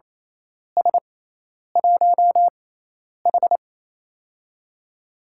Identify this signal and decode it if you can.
Morse code, heard as a keyed tone, about 27 wpm, spaced as for 10 wpm, 700 Hz. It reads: S1H
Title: USA Call Signs - 4 Characters - 6x Character Spacing - 28wpm